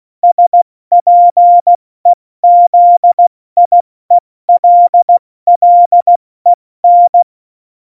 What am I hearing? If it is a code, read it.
Morse code, heard as a keyed tone, about 16 wpm, spaced as for 14 wpm, 700 Hz. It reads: SPEZIELLEN
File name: Morsekurs / Witze / Witz 14 Juli 2020, 70 F80